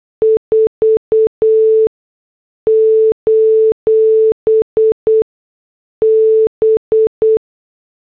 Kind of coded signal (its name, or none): Morse code